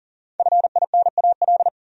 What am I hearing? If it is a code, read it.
Morse code, heard as a keyed tone, about 50 wpm, 700 Hz. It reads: FINAL
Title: Speed-Racing Top 500 Words 40wpm